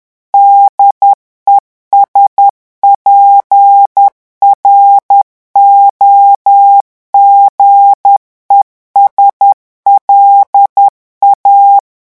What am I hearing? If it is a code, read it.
Morse code, heard as a keyed tone, about 10 wpm, 800 Hz. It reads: DESPROGES LA
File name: revision9 citations humoristiques vitesse 10 mots minute